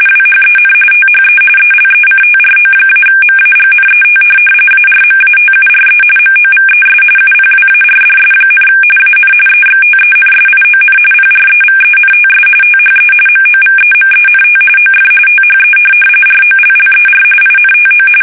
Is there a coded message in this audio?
Yes: RTTY from 0.1 to 18.1 s, about 45.45 baud. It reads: WHEN THE ELECTRO-MAGNET ARMATURE MOVES UPWARDS, AN EXTENSION PREVENTS THE MOVEMENT OF THE ROLLER INTO